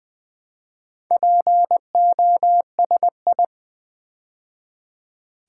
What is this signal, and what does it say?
Morse code, heard as a keyed tone, about 20 wpm, 700 Hz. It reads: POSI
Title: Build A Word - 2 Letters At A Time - 8-Letter Words - 20wpm